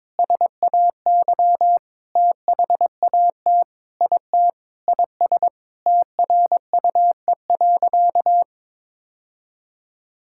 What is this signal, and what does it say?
Morse code, heard as a keyed tone, about 22 wpm, 700 Hz. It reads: SAY THAT IT IS TRUE.
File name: practice / 22wpm / Sentences with 8-Word Phrases - 22wpm